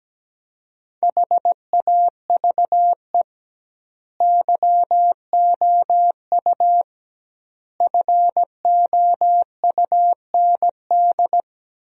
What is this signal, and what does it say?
Morse code, heard as a keyed tone, about 17 wpm, 700 Hz. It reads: HAVE YOU FOUND
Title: 2x Word-Spacing - Sentences from Top 300 Words - 17wpm